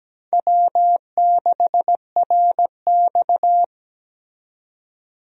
Morse code, heard as a keyed tone, about 17 wpm, 700 Hz. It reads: W6RX
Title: USA Call Signs - 4 Characters - 17wpm